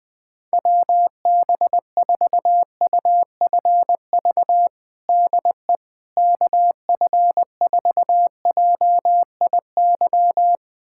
Morse code, heard as a keyed tone, about 21 wpm, 700 Hz. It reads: WB4UFV DE KF4JIY
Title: USA Callsign DE Callsign - 6 Characters - 20wpm